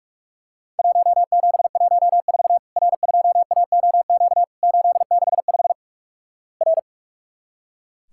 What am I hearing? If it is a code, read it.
Morse code, heard as a keyed tone, about 46 wpm, 700 Hz. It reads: 1714 R2AOQ 865 R
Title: Worked All Europe DX Contest - QTC Exchange - 45wpm